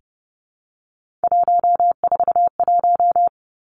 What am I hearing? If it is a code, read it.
Morse code, heard as a keyed tone, about 30 wpm, 700 Hz. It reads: 141
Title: Three Challenging Letter-Number Combinations 30wpm